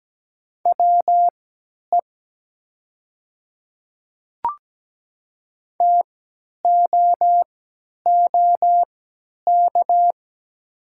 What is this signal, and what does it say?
Morse code, heard as a keyed tone, about 17 wpm, spaced as for 10 wpm, 690 Hz. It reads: WE TOOK
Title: 3x Character Spacing - Top 300 Words - 17wpm